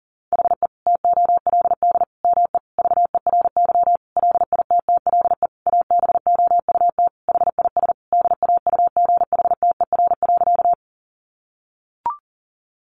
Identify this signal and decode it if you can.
Morse code, heard as a keyed tone, about 40 wpm, 700 Hz. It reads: HE TOLD ME VERY LITTLE ABOUT HIS DAUGHTER.
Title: Sentences from Top 1500 Words 40wpm